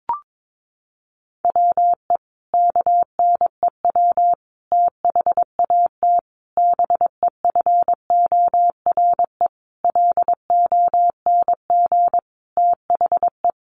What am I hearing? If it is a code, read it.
Morse code, heard as a keyed tone, about 22 wpm, 700 Hz. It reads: WE KNEW THAT BEFORE LONG THE
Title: Sentences from Top 400 Words 22wpm